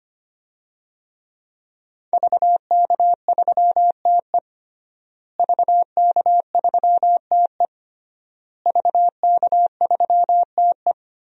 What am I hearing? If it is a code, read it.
Morse code, heard as a keyed tone, about 25 wpm, 700 Hz. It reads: VK3TE VK3TE VK3TE
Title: DXCC-100 Call Signs - 5 Characters - Repeated 3x - 25wpm